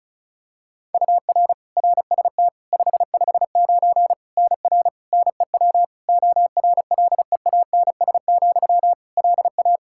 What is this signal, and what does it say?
Morse code, heard as a keyed tone, about 34 wpm, 700 Hz. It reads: UR RST 559 NR NEW ORLEANS, LA
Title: K1USN SST QSO Practice 35wpm